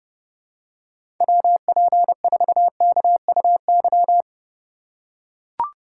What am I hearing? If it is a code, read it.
Morse code, heard as a keyed tone, about 30 wpm, 700 Hz. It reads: WP4KUY
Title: USA Call Signs - 2x Character Spacing - 30wpm